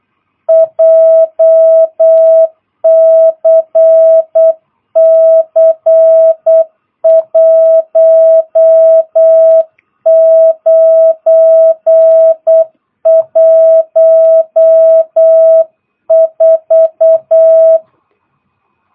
Morse code, heard as a keyed tone, about 8 wpm, 660 Hz. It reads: JCC 1914
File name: jh8cbh / 2022cw